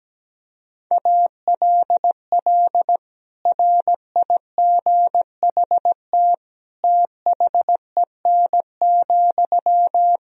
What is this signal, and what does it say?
Morse code, heard as a keyed tone, about 17 wpm, 700 Hz. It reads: ALL RIGHT THEN,
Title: Sentences from Top 500 Words and Abbreviations 17wpm